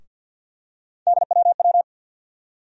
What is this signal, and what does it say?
Morse code, heard as a keyed tone, about 51 wpm, 700 Hz. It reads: DWW
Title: USA Call Sign Postfixes - 3 Letters 50wpm